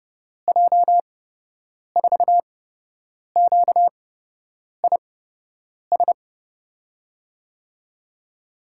Morse code, heard as a keyed tone, about 30 wpm, spaced as for 8 wpm, 700 Hz. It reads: J4QIS
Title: USA Call Signs - 6 Characters - 8x Character Spacing - 30wpm